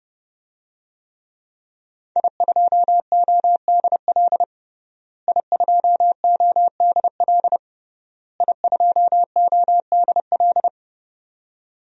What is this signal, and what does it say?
Morse code, heard as a keyed tone, about 30 wpm, 700 Hz. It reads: I2ODL I2ODL I2ODL
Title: DXCC-100 Call Signs - 5 Characters - Repeated 3x - 30wpm